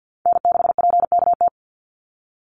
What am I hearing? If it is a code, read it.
Morse code, heard as a keyed tone, about 50 wpm, 700 Hz. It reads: N6PKT
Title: NA SOTA Chaser Callsigns 50wpm